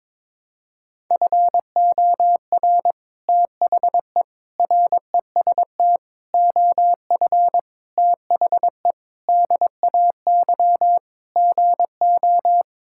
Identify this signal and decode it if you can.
Morse code, heard as a keyed tone, about 22 wpm, 700 Hz. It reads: FOR THE REST OF THE DAY GO
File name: practice / 22wpm / Sentences with 3-Word Phrases - 22wpm